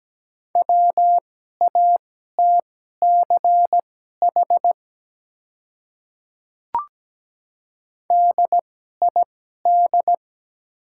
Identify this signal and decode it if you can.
Morse code, heard as a keyed tone, about 17 wpm, spaced as for 12 wpm, 700 Hz. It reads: WATCH DID
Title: 2x Character Spacing - Top 300 Words - 17wpm